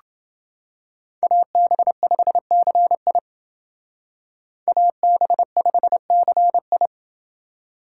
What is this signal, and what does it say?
Morse code, heard as a keyed tone, about 28 wpm, 700 Hz. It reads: AB5CI AB5CI